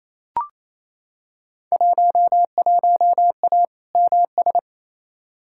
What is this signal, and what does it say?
Morse code, heard as a keyed tone, about 28 wpm, 700 Hz. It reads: 11A MS
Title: Field Day Exchanges Encore - 28wpm